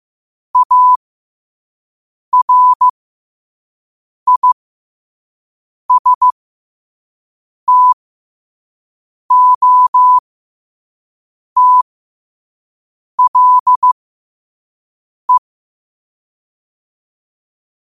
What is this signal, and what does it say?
Morse code, heard as a keyed tone, about 15 wpm, spaced as for 5 wpm, 1000 Hz. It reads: ARISTOTLE
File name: qotdfiles / 5wpm